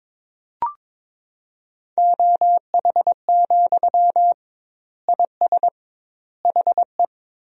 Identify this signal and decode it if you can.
Morse code, heard as a keyed tone, about 22 wpm, 700 Hz. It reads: OH, IS HE
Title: 2x Word-Spacing - Sentences from Top 500 Words - 22wpm